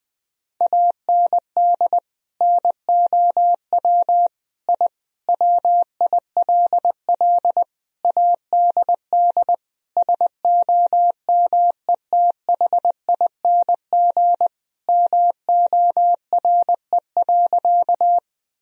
Morse code, heard as a keyed tone, about 21 wpm, 700 Hz. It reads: AND NOW I WILL ADD SOMETHING MORE.
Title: Sentences from Top 1000 Words - Review Part 1 - 20wpm